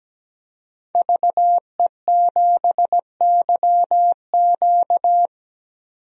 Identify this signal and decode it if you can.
Morse code, heard as a keyed tone, about 17 wpm, 700 Hz. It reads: VE7YQ
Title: DXCC-100 Call Signs - 5 Characters - 17wpm